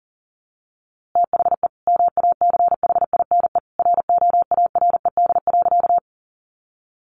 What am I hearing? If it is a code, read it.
Morse code, heard as a keyed tone, about 40 wpm, 700 Hz. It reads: THE MACHINE ROARED.
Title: A Sound Of Thunder 40wpm